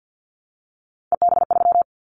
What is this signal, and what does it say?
Morse code, heard as a keyed tone, about 50 wpm, 700 Hz. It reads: EB3
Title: Callsign Prefixes for Top 25 DXCC - Repeated 3x - 50wpm